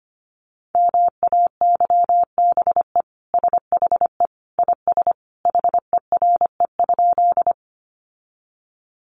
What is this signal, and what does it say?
Morse code, heard as a keyed tone, about 25 wpm, 700 Hz. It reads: MAYBE SHE IS HERE?
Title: Sentences from Top 1000 Words 25wpm